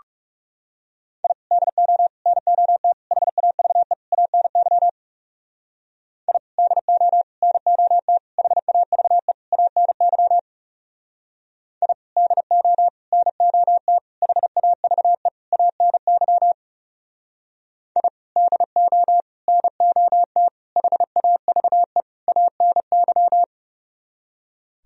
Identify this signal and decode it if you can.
Morse code, heard as a keyed tone, about 36 wpm, 700 Hz. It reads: I DO NOT HAVE ANY I DO NOT HAVE ANY I DO NOT HAVE ANY I DO NOT HAVE ANY
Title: Speed-Racing - Sets of 5 Words 30wpm